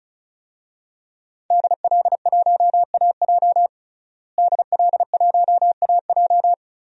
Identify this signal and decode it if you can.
Morse code, heard as a keyed tone, about 34 wpm, 700 Hz. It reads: DL1AJ DL1AJ
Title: DXCC-100 Call Signs - 5 Characters - Repeated 3x - 35wpm